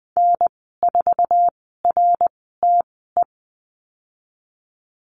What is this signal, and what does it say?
Morse code, heard as a keyed tone, about 20 wpm, spaced as for 15 wpm, 700 Hz. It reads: N4RTE
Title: 055 Lesson 10g - Letter r - Call Signs 20wpm